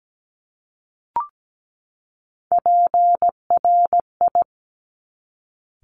Morse code, heard as a keyed tone, about 17 wpm, 700 Hz. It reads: PRI
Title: Top Three Letter Combinations in Words 17wpm